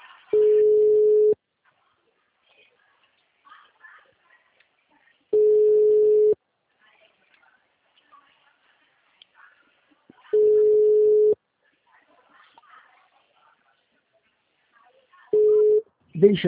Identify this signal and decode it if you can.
Ringback tone (Europe, 425 Hz), with speech.
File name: out- -2020- - - .73